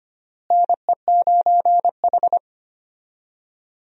Morse code, heard as a keyed tone, about 25 wpm, 700 Hz. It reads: NE9H
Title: USA Call Signs - 4 Characters - Repeated 3x - 25wpm